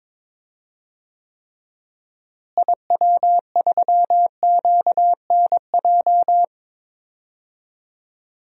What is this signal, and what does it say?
Morse code, heard as a keyed tone, about 22 wpm, 700 Hz. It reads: IW3QNJ
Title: DXCC-100 Call Signs - 6 Characters - Repeated 3x - 22wpm